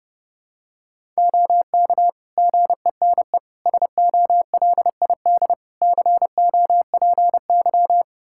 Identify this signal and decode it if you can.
Morse code, heard as a keyed tone, about 30 wpm, 700 Hz. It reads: OK GENE SOLID COPY
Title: Practice QSO - Exchange 3 - More Info - 30wpm